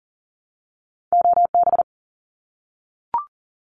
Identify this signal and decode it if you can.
Morse code, heard as a keyed tone, about 40 wpm, 700 Hz. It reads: OB
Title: Abbreviations Repeated 3x - 40wpm